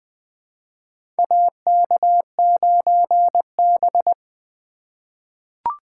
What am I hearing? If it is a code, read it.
Morse code, heard as a keyed tone, about 20 wpm, 700 Hz. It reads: AK9B
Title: USA Call Signs - 4 Characters - 3x Character Spacing - 20wpm